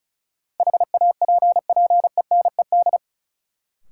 Morse code, heard as a keyed tone, about 36 wpm, 700 Hz. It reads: HAPPENED
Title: Sentences from Top 1000 Words - Review Part 2 - New Word Drill - 35wpm